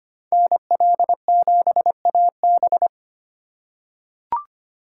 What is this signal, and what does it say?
Morse code, heard as a keyed tone, about 25 wpm, 700 Hz. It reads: NL7AB
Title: USA Call Signs - 5 Characters - 3x Character Spacing - 25wpm